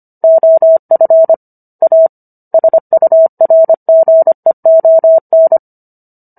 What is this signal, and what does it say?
Morse code, heard as a keyed tone, about 25 wpm, 650 Hz. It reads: OF A SURGEON